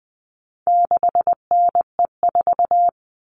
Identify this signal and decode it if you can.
Morse code, heard as a keyed tone, about 20 wpm, 700 Hz. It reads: 6NE4
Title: All Cut Numbers - 4-Digit 20wpm